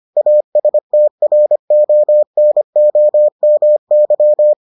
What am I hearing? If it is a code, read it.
Morse code, heard as a keyed tone, about 25 wpm, 600 Hz. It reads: ASTRONOMY